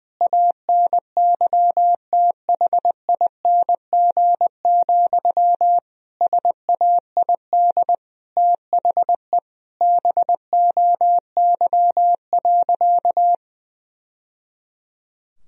Morse code, heard as a keyed tone, about 21 wpm, 700 Hz. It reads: ANYTHING, SAID THE BOY.
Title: Sentences from Top 1000 Words - Review Part 4 - 20wpm